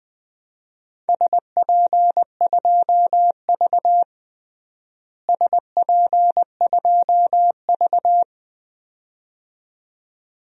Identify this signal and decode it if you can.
Morse code, heard as a keyed tone, about 20 wpm, 700 Hz. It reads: SP2V SP2V
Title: DXCC-100 Call Signs - 4 Characters - Repeated 2x - 20wpm